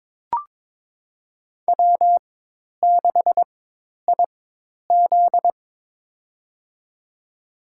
Morse code, heard as a keyed tone, about 22 wpm, spaced as for 10 wpm, 700 Hz. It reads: W6IZ
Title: USA Call Signs - 4 Characters - 4x Character Spacing - 22wpm